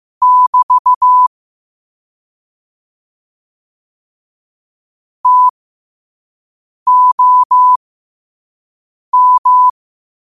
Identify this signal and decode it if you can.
Morse code, heard as a keyed tone, about 15 wpm, spaced as for 5 wpm, 1000 Hz. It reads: <BT> TOM
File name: qotdfiles / 5wpm